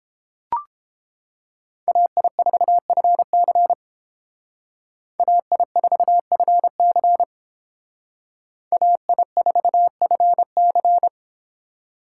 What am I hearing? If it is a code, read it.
Morse code, heard as a keyed tone, about 29 wpm, 700 Hz. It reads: AI4FC AI4FC AI4FC